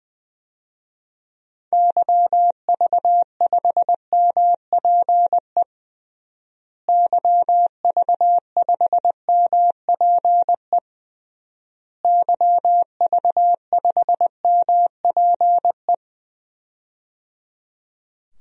Morse code, heard as a keyed tone, about 20 wpm, 700 Hz. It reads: YV5MPE YV5MPE YV5MPE
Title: DXCC-100 Call Signs - 6 Characters - Repeated 3x - 20wpm